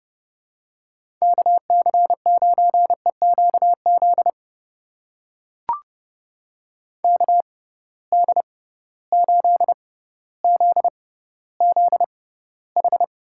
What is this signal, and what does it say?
Morse code, heard as a keyed tone, about 30 wpm, spaced as for 15 wpm, 700 Hz. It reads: KC9EQZ KD8ZZH
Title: USA Call Signs - 6 Characters - 6x Character Spacing - 30wpm